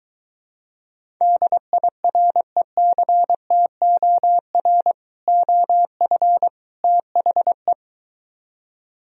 Morse code, heard as a keyed tone, about 23 wpm, 700 Hz. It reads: DIRECTOR OF THE